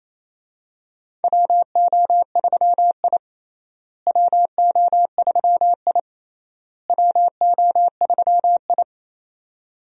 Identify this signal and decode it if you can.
Morse code, heard as a keyed tone, about 28 wpm, 700 Hz. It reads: WO3I WO3I WO3I
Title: USA Call Signs - 4 Characters - Repeated 3x - 28wpm